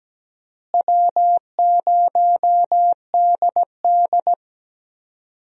Morse code, heard as a keyed tone, about 17 wpm, 700 Hz. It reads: W0DD
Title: USA Call Signs - 4 Characters - Repeated 2x - 17wpm